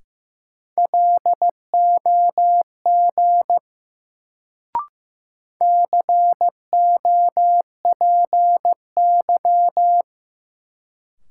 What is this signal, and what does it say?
Morse code, heard as a keyed tone, about 15 wpm, 700 Hz. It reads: LOG COPY
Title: Sentences from Top 1400 Words - New Word Drill - 15wpm